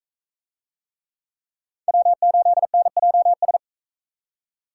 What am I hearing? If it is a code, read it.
Morse code, heard as a keyed tone, about 38 wpm, 700 Hz. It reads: W8NJS